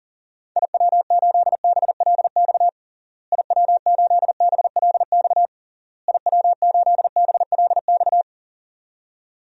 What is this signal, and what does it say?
Morse code, heard as a keyed tone, about 40 wpm, 700 Hz. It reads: IW8BLX IW8BLX IW8BLX
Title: DXCC-100 Call Signs - 6 Characters - Repeated 3x - 40wpm